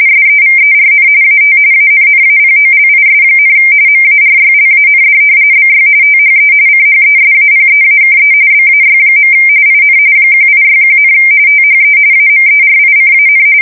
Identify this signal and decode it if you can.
RTTY, about 45.45 baud. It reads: SEVERN AT ANNAPOLIS, MARYLAND BY THE CHESAPEAKE BAY. IN 1861, THE AMERICAN
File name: fsk170 / usnavy1